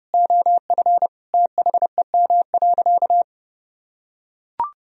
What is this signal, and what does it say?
Morse code, heard as a keyed tone, about 30 wpm, 700 Hz. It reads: OF THEM.
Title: Sentences from Top 1000 Words - Review Part 2 - 30wpm